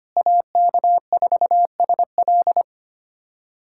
Morse code, heard as a keyed tone, about 25 wpm, 700 Hz. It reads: AK4SL
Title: USA Callsign DE Callsign - 5 Characters - 25wpm